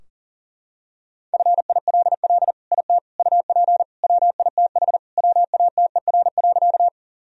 Morse code, heard as a keyed tone, about 40 wpm, 700 Hz. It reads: FILL IT UP WITH WATER.
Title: Sentences from Top 1000 Words - Review Part 3 - 40wpm